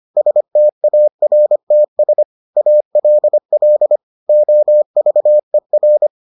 Morse code, heard as a keyed tone, about 25 wpm, 600 Hz. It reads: STARTS ALL OVER